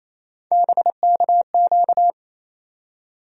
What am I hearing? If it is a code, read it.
Morse code, heard as a keyed tone, about 28 wpm, 700 Hz. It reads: BKQ
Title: USA Call Sign Postfixes - 3 Letters 28wpm